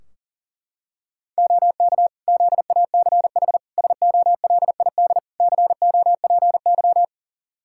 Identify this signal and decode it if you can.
Morse code, heard as a keyed tone, about 40 wpm, 700 Hz. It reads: OK ZACH SOLID COPY
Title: K1USN SST QSO Practice 40wpm